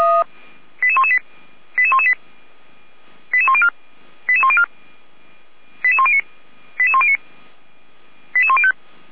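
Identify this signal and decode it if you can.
Five-tone selective call (ZVEI), heard as quick runs of five tones, tones of about 70 ms. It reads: 88118 88118 88104 88104 88109 88109 88105
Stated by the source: ZVEI five-tone call 88102 received by SDR, from the audio server of Feuerwehr Orlamünde.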